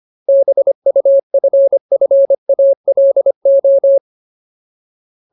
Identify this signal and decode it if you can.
Morse code, heard as a keyed tone, about 25 wpm, 550 Hz. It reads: BUFFALO
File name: Ultimate Word List / 25WPM Ultimate